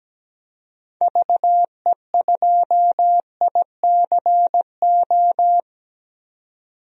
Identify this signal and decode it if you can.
Morse code, heard as a keyed tone, about 17 wpm, 700 Hz. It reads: VE2ICO
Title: DXCC-100 Call Signs - 6 Characters - 17wpm